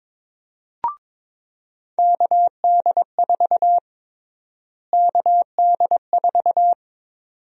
Morse code, heard as a keyed tone, about 21 wpm, 700 Hz. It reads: KD4 KD4
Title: USA Call Sign Prefixes - Repeated 2x - 22wpm